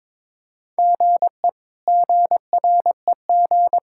Morse code, heard as a keyed tone, about 21 wpm, 700 Hz. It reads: GE GREG
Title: K1USN SST QSO Practice - 22wpm